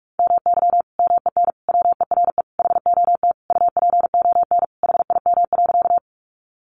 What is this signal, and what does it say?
Morse code, heard as a keyed tone, about 44 wpm, 700 Hz. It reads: MY MEN WERE HOT UPON HIM.
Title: Sentences from Top 1100 Words 45wpm